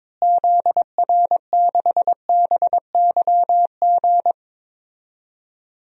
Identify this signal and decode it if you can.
Morse code, heard as a keyed tone, about 22 wpm, 700 Hz. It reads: ZR6BYG
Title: DXCC-100 Call Signs - 6 Characters - Repeated 2x - 22wpm